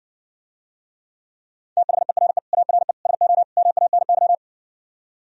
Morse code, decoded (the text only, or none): THERE ARE SO MANY